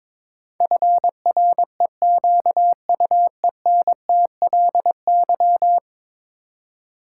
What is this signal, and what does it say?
Morse code, heard as a keyed tone, about 22 wpm, 700 Hz. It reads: FREQUENTLY
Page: morse-code-ninja-practice-22wpm